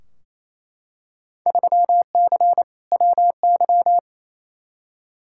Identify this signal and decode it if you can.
Morse code, heard as a keyed tone, about 28 wpm, 700 Hz. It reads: 3C WY
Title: Field Day Exchanges Encore - 28wpm